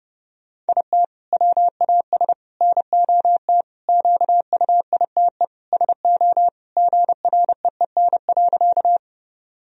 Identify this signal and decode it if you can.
Morse code, heard as a keyed tone, about 30 wpm, 700 Hz. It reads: IT WAS NOT QUITE SO GREEN.
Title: Sentences from Top 1000 Words - Review Part 2 - 30wpm